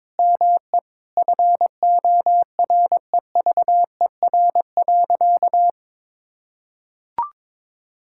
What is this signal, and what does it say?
Morse code, heard as a keyed tone, about 22 wpm, 700 Hz. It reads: ME FOREVER.
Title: Sentences from Top 1600 Words 22wpm